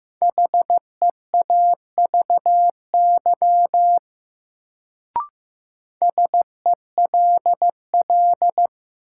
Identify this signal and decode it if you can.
Morse code, heard as a keyed tone, about 15 wpm, 700 Hz. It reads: HEAVY SELL
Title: Sentences from Top 1000 Words - Review Part 6 - New Word Drill - 15wpm